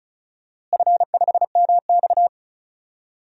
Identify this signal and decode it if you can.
Morse code, heard as a keyed tone, about 34 wpm, 700 Hz. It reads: F5MX
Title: DXCC-100 Call Signs - 4 Characters - Repeated 3x - 35wpm